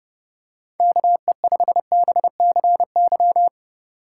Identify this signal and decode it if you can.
Morse code, heard as a keyed tone, about 30 wpm, 700 Hz. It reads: KE5BCY
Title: K1USN SST QSO Practice - 30wpm